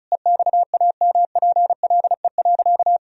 Morse code, heard as a keyed tone, about 34 wpm, 700 Hz. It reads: EXAMPLE.
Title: Sentences from Top 1000 Words - Review Part 1 - 35wpm